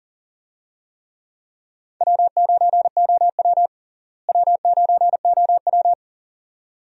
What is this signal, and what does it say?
Morse code, heard as a keyed tone, about 40 wpm, 700 Hz. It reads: W9OW W9OW
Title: USA Call Signs - 4 Characters - Repeated 2x - 40wpm